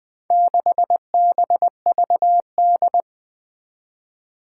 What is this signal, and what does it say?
Morse code, heard as a keyed tone, about 20 wpm, 700 Hz. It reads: 6BVD
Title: Four Challenging Letter-Number Combinations 20wpm